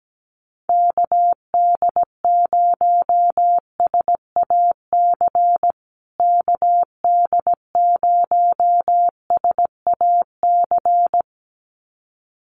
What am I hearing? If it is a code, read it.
Morse code, heard as a keyed tone, about 17 wpm, 700 Hz. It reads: KD0SAC KD0SAC
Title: USA Call Signs Repeated 2x 17wpm